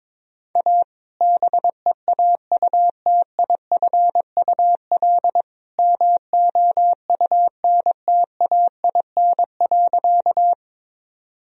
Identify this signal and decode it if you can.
Morse code, heard as a keyed tone, about 22 wpm, 700 Hz. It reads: A BEAUTIFUL MOUNTAIN.
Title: Sentences from Top 1000 Words - Review Part 1 - 22wpm